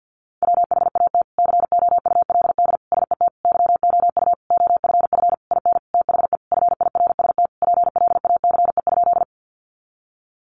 Morse code, heard as a keyed tone, about 50 wpm, 700 Hz. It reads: WHAT COULD SET YOU OFF IN THE FIRST PLACE?
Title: Sentences from Top 200 Words 50wpm